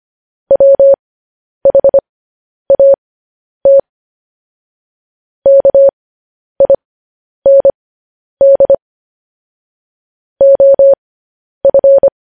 Morse code, heard as a keyed tone, about 25 wpm, spaced as for 10 wpm, 550 Hz. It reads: WHAT KIND OF
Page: news headlines in morse code at 10 wpm